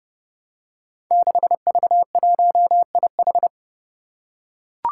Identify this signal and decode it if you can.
Morse code, heard as a keyed tone, about 30 wpm, 700 Hz. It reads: 6V1IH
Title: Five Challenging Letter-Number Combinations 30wpm